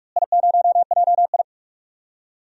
Morse code, heard as a keyed tone, about 41 wpm, 700 Hz. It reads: I0JI
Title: DXCC-100 Call Signs - 4 Characters - Repeated 3x - 45wpm